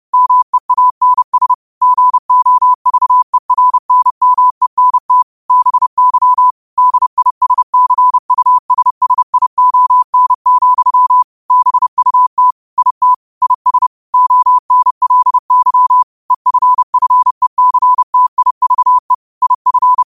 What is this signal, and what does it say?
Morse code, heard as a keyed tone, about 30 wpm, 1000 Hz. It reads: MEANS GOVERNMENT BY DISCUSSION, BUT IT IS ONLY EFFECTIVE IF
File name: qotdfiles / 30wpm